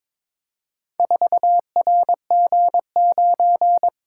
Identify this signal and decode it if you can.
Morse code, heard as a keyed tone, about 22 wpm, 700 Hz. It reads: 4RG9
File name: practice / 22wpm / Four Letter-Number Combinations - 22wpm